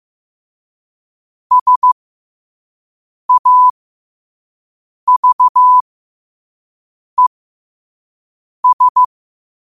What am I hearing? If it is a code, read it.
Morse code, heard as a keyed tone, about 15 wpm, spaced as for 5 wpm, 1000 Hz. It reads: SAVES
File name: qotdfiles / 5wpm